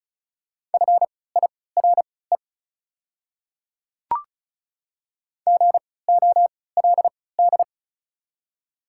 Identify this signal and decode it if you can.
Morse code, heard as a keyed tone, about 34 wpm, spaced as for 20 wpm, 690 Hz. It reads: FIRE GOLD
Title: 3x Character Spacing - Top 500 Words - 35wpm